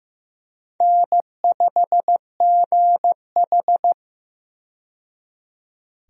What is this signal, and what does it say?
Morse code, heard as a keyed tone, about 16 wpm, 700 Hz. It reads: N5GH
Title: USA Call Signs - 4 Characters - 15wpm